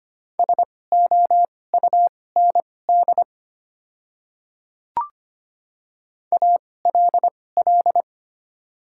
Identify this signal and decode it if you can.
Morse code, heard as a keyed tone, about 25 wpm, spaced as for 18 wpm, 700 Hz. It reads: SOUND ALL
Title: 2x Character Spacing - Top 100 Words - 25wpm